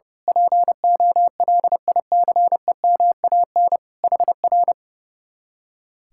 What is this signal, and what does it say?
Morse code, heard as a keyed tone, about 28 wpm, 700 Hz. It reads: POLICEMAN HR
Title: Speed-Racing Common QSO Phrases - 30wpm